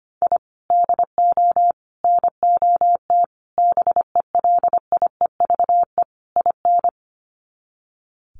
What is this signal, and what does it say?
Morse code, heard as a keyed tone, about 25 wpm, 700 Hz. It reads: I DO NOT BELIEVE IN
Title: Sets of 5 Words Encore 25wpm